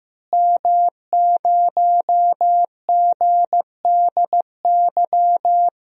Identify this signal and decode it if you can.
Morse code, heard as a keyed tone, about 15 wpm, 700 Hz. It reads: M0GDY
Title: DXCC-100 Call Signs - 5 Characters - Repeated 3x - 15wpm